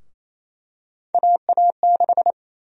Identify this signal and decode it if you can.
Morse code, heard as a keyed tone, about 28 wpm, 700 Hz. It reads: AA6
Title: USA Call Sign Prefixes - 28wpm